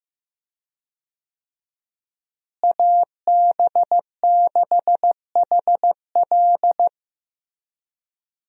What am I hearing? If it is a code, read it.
Morse code, heard as a keyed tone, about 15 wpm, 700 Hz. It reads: AB6HL
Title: USA Call Signs - 5 Characters - Repeated 3x - 15wpm